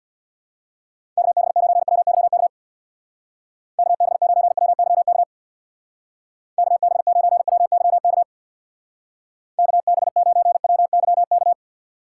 Morse code, heard as a keyed tone, about 62 wpm, 700 Hz. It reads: KB9WYK KB9WYK KB9WYK KB9WYK
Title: Speed Racing - USA 2x3 Call Signs 50wpm